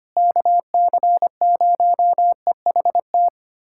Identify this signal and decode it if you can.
Morse code, heard as a keyed tone, about 25 wpm, 700 Hz. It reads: KC0EHT
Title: USA Callsign DE Callsign 25wpm